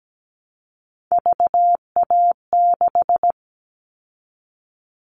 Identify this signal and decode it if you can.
Morse code, heard as a keyed tone, about 17 wpm, 700 Hz. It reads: VA6
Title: Callsign Prefixes for Top 100 DXCC - 17wpm